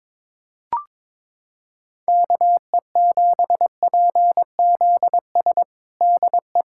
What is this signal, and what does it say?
Morse code, heard as a keyed tone, about 22 wpm, 700 Hz. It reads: KE7PZS DE
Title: USA Callsign DE Callsign 22wpm